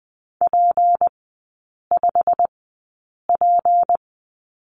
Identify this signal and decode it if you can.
Morse code, heard as a keyed tone, about 20 wpm, spaced as for 8 wpm, 700 Hz. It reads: P5P
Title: 152 Lesson 25e - Letter p - Three Characters 20wpm